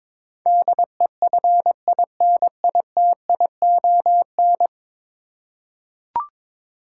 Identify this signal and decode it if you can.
Morse code, heard as a keyed tone, about 22 wpm, 700 Hz. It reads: DEFINITION
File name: 10 Letter Words - First 3 Letters Given - 22wpm